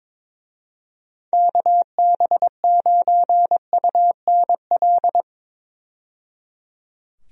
Morse code, heard as a keyed tone, about 22 wpm, 700 Hz. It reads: KB9UNL
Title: USA Call Signs - 6 Characters - 22wpm